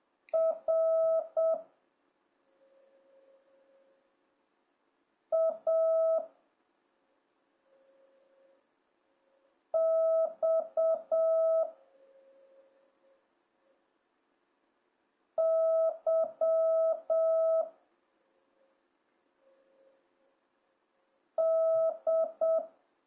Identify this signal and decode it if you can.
Morse code, heard as a keyed tone, about 7 wpm, 650 Hz. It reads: R A X Y D